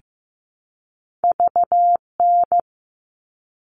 Morse code, heard as a keyed tone, about 15 wpm, 700 Hz. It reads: VN
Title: All Cut Numbers - 2-Digit 15wpm